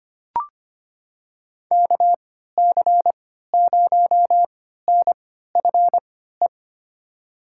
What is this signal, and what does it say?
Morse code, heard as a keyed tone, about 25 wpm, spaced as for 14 wpm, 700 Hz. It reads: KC0NFE
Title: USA Call Signs - 3x Character Spacing - 25wpm